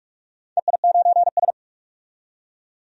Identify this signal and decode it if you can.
Morse code, heard as a keyed tone, about 44 wpm, 690 Hz. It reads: EI9S
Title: DXCC-100 Call Signs - 4 Characters - 45wpm